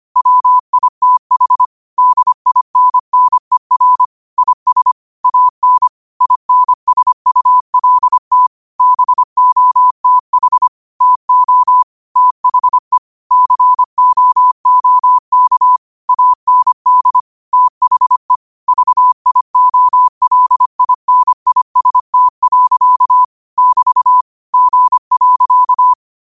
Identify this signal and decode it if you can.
Morse code, heard as a keyed tone, about 25 wpm, 1000 Hz. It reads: WITH DINNER IS AN INSULT BOTH TO THE COOK AND THE VIOLINIST. <BT> G.
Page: Quotes for Sat, 16 Aug 2025 in Morse Code at 25 words per minute.